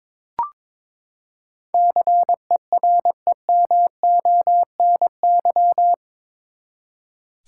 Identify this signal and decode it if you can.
Morse code, heard as a keyed tone, about 22 wpm, 700 Hz. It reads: CEREMONY
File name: practice / 22wpm / Eight Letter Words - 22wpm